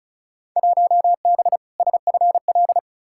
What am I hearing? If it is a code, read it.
Morse code, heard as a keyed tone, about 34 wpm, 700 Hz. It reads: 1B SFL
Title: Field Day Exchanges Encore - 35wpm